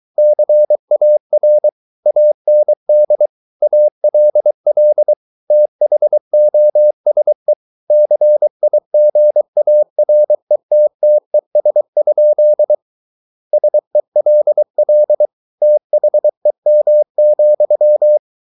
Morse code, heard as a keyed tone, about 23 wpm, 600 Hz. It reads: CAR AND ALL THOSE CIGARETTES? SELL THEM,